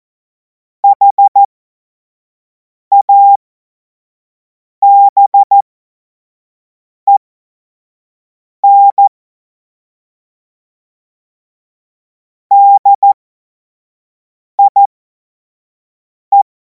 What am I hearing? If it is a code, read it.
Morse code, heard as a keyed tone, about 14 wpm, spaced as for 5 wpm, 800 Hz. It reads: HABEN DIE
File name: Morsekurs / Witze / Witz 16. Juni 2026 25 F70